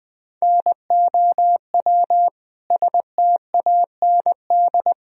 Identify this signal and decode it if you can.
Morse code, heard as a keyed tone, about 20 wpm, 700 Hz. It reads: NOW STAND
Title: Sentences from Top 1000 Words - Review Part 1 - 20wpm